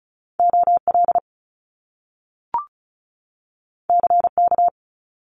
Morse code, heard as a keyed tone, about 34 wpm, 700 Hz. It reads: OL CK
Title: Top Two Letter Combinations in Words 35wpm